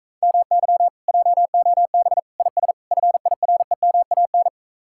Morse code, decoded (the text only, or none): MY JOB IS FIREMAN